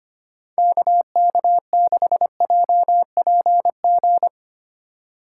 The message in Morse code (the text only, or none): KK6JPG